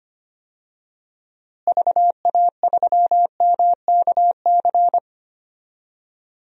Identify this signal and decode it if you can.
Morse code, heard as a keyed tone, about 25 wpm, 700 Hz. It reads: VA3MKC
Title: DXCC-100 Call Signs - 6 Characters - Repeated 3x - 25wpm